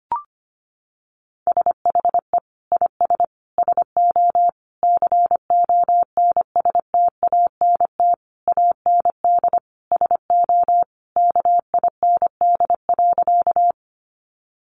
Morse code, heard as a keyed tone, about 25 wpm, 700 Hz. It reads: SHE IS SO CONSTANT AND SO KIND.
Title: Sentences from Top 1500 Words 25wpm